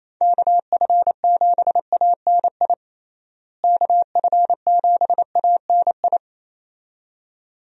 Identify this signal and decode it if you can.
Morse code, heard as a keyed tone, about 28 wpm, 700 Hz. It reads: KF7ANI KF7ANI
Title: USA Call Signs - 6 Characters - Repeated 2x - 28wpm